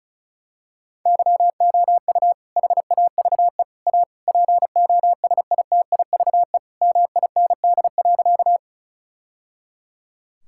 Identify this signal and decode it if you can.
Morse code, heard as a keyed tone, about 36 wpm, 700 Hz. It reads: YOU HAVE A POSITIVE MIND.
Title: Sentences from Top 1000 Words - Review Part 5 - 35wpm